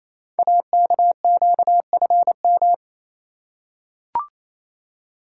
Morse code, heard as a keyed tone, about 28 wpm, 700 Hz. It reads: AKQFM
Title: Code Groups - 5 Letter:Numbers - 28wpm